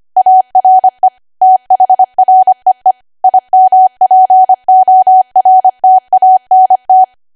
Morse code, heard as a keyed tone, about 25 wpm, 750 Hz. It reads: ARE THREE IMPORTANT
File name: cwa / W1AW-Level3 / 25WPM